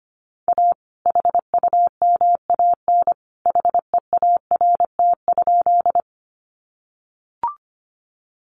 Morse code, heard as a keyed tone, about 25 wpm, 700 Hz. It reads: A HUMAN HEART?
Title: Sentences from Top 1200 Words 25wpm